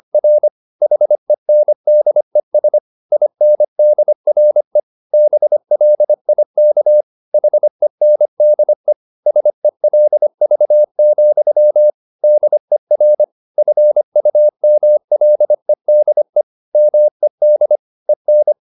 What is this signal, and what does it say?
Morse code, heard as a keyed tone, about 25 wpm, 600 Hz. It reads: R HENDES INDRE BLIK HENDE SELV, DER FUMLEDE MED EN
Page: Ferie 25 wpm